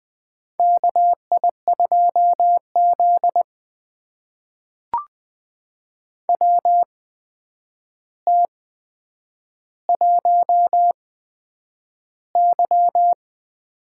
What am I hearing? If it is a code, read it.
Morse code, heard as a keyed tone, about 20 wpm, spaced as for 9 wpm, 700 Hz. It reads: KI2Z WT1Y
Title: USA Call Signs - 4 Characters - 8x Character Spacing - 20wpm